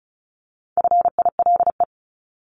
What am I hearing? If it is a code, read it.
Morse code, heard as a keyed tone, about 34 wpm, 700 Hz. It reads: FILE
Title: Top 1001-1250 Words 35wpm